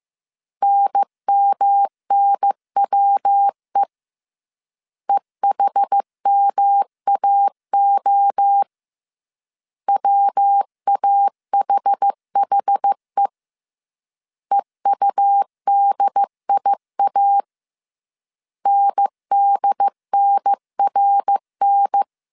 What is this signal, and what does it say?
Morse code, heard as a keyed tone, about 15 wpm, 800 Hz. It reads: NMNWE EHMAO WAHHE EUDIA NDNRN